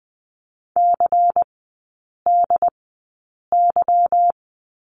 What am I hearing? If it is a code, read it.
Morse code, heard as a keyed tone, about 20 wpm, spaced as for 8 wpm, 700 Hz. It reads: CDY
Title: 145 Lesson 24e - Letter y - Three Characters 20wpm